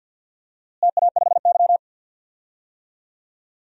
Morse code, heard as a keyed tone, about 52 wpm, 700 Hz. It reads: TA5Y
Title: DXCC-100 Call Signs - 4 Characters - Repeated 2x - 50wpm